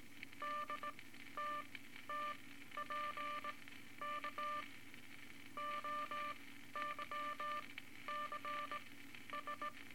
Morse code, heard as a keyed tone, about 18 wpm, spaced as for 12 wpm, 1180 Hz, with noise. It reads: DTTPK OYCS
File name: Morse Code Records / 18 A LCM 164 Ten Words Per Minute